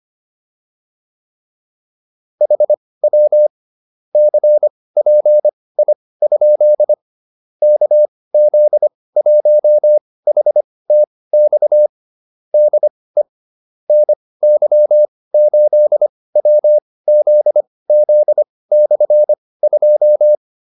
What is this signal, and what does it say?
Morse code, heard as a keyed tone, about 25 wpm, spaced as for 18 wpm, 600 Hz. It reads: HW CPI? KZ1HTX DE NY8WZZ/2